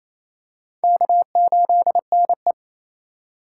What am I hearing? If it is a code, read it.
Morse code, heard as a keyed tone, about 27 wpm, 700 Hz. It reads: K8NE
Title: USA Call Signs - 4 Characters - 3x Character Spacing - 28wpm